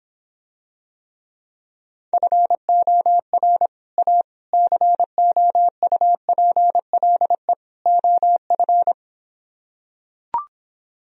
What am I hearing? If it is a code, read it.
Morse code, heard as a keyed tone, about 26 wpm, 700 Hz. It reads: FOR A COUPLE OF